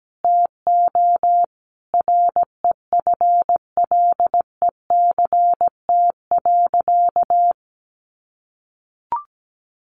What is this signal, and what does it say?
Morse code, heard as a keyed tone, about 17 wpm, 700 Hz. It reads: TO REFLECT.
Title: Sentences from Top 1500 Words 17wpm